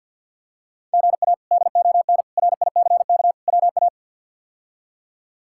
Morse code, heard as a keyed tone, about 53 wpm, 700 Hz. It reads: GA DON RICK WA
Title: K1USN SST QSO Practice - 50wpm